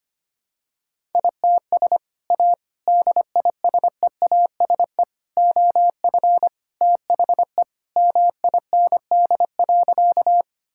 Morse code, heard as a keyed tone, about 25 wpm, 700 Hz. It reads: ITS A DISEASE OF THE MIND.
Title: Sentences from Top 1000 Words - Review Part 6 - 25wpm